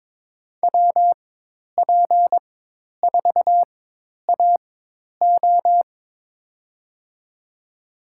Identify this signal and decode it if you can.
Morse code, heard as a keyed tone, about 22 wpm, spaced as for 10 wpm, 700 Hz. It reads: WP4AO
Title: USA Call Signs - 5 Characters - 4x Character Spacing - 22wpm